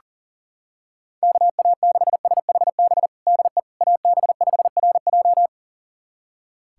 Morse code, heard as a keyed tone, about 37 wpm, 700 Hz. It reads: KA6SHB DE AB5RJ
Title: USA Callsign DE Callsign 40wpm